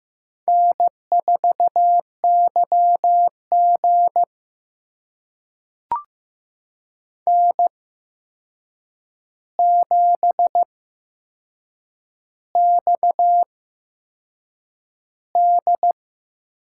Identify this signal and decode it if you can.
Morse code, heard as a keyed tone, about 15 wpm, spaced as for 6 wpm, 700 Hz. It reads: N4YG N7XD
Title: USA Call Signs - 4 Characters - 8x Character Spacing - 15wpm